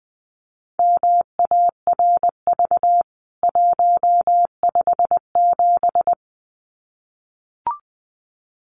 Morse code, heard as a keyed tone, about 20 wpm, 700 Hz. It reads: MARV 157
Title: CWT Exchange 20wpm